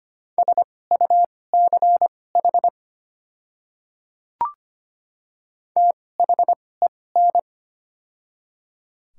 Morse code, heard as a keyed tone, about 25 wpm, spaced as for 18 wpm, 700 Hz. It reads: SUCH THEN
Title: 2x Character Spacing - Top 200 Words - 25wpm